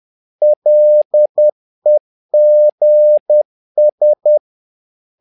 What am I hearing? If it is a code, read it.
Morse code, heard as a keyed tone, about 10 wpm, 600 Hz. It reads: LEGS
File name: learning center / tower trouble / tower trouble 10wpm